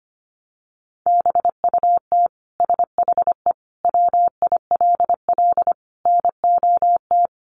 Morse code, heard as a keyed tone, about 25 wpm, 700 Hz. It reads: BUT SHE WILL NOT
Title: Sentences from Top 1200 Words 25wpm